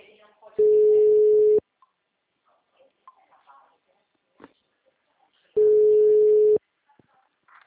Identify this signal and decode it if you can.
Ringback tone (Europe, 425 Hz), with speech.